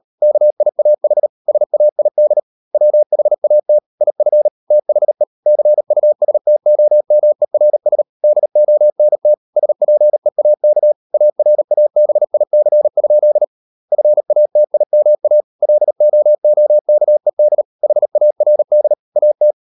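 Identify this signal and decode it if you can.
Morse code, heard as a keyed tone, about 38 wpm, 600 Hz. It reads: KIAH SAID WHAT IF THE CUSTOMERS DONT SPEAK ARABIC? FATIMA LOOKED HARD AT
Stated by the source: Morse code practice at 38 WPM, sending Ken Follett's novel Never in English, from CW med Gnister.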